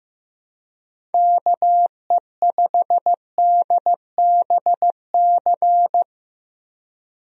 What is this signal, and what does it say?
Morse code, heard as a keyed tone, about 15 wpm, 700 Hz. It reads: KE5DBC
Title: USA Call Signs - 6 Characters - Repeated 2x - 15wpm